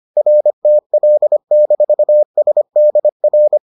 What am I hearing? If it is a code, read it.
Morse code, heard as a keyed tone, about 25 wpm, 600 Hz. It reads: RTL-SDR